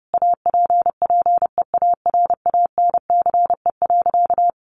Morse code, heard as a keyed tone, about 30 wpm, 700 Hz. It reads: APPEARANCE.
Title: Sentences from Top 1100 Words 30wpm